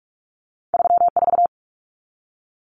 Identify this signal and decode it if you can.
Morse code, heard as a keyed tone, about 46 wpm, 700 Hz. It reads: 34
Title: Numbers - 2-Digit 45wpm